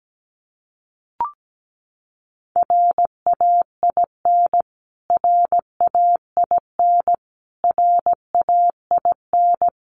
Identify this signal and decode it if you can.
Morse code, heard as a keyed tone, about 17 wpm, 700 Hz. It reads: RAIN RAIN RAIN
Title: Top 500 Words Repeated 17wpm